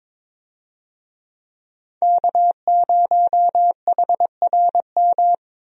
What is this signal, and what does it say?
Morse code, heard as a keyed tone, about 22 wpm, 700 Hz. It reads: K0HRM
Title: USA Call Signs - 5 Characters - Repeated 2x - 22wpm